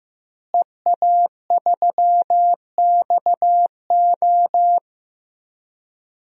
Morse code, heard as a keyed tone, about 15 wpm, 700 Hz. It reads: EA3XO
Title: DXCC-100 Call Signs - 5 Characters - Repeated 3x - 15wpm